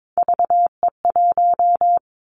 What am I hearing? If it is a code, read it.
Morse code, heard as a keyed tone, about 22 wpm, 700 Hz. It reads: VE1
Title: Callsign Prefixes for Top 100 DXCC - 22wpm